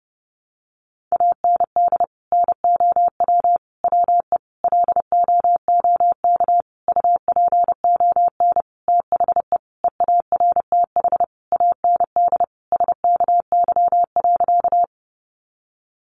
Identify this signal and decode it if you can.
Morse code, heard as a keyed tone, about 28 wpm, 700 Hz. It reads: AND NOW WE LOOK UPON THE EARTH AND SKY.
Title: Sentences from Top 1200 Words 30wpm